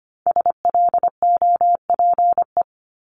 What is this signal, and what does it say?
Morse code, heard as a keyed tone, about 25 wpm, 700 Hz. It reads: SLOPE
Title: Multi-letter Phonemes Repeated 3x 25wpm